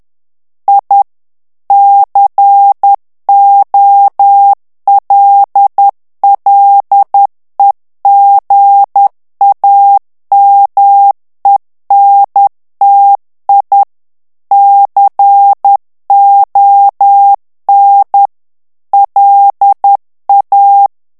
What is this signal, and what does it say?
Morse code, heard as a keyed tone, about 10 wpm, 800 Hz. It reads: I COLLEGAMENTI CON LA
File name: morse mp3 / ITALIEN texte clair 1 vitesse 10 mots minute